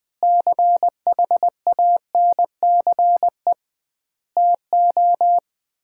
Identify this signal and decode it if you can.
Morse code, heard as a keyed tone, about 20 wpm, 700 Hz. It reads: CHANCE TO
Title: 2x Word-Spacing - Sets of 4 Words - 20wpm